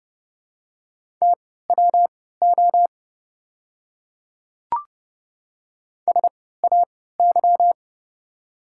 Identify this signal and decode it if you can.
Morse code, heard as a keyed tone, about 30 wpm, spaced as for 17 wpm, 700 Hz. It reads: TWO SAY
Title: 3x Character Spacing - Top 200 Words - 30wpm